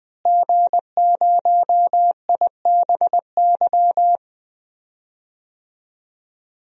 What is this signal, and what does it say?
Morse code, heard as a keyed tone, about 20 wpm, 700 Hz. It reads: G0IBY
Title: DXCC-100 Call Signs - 5 Characters - Repeated 2x - 20wpm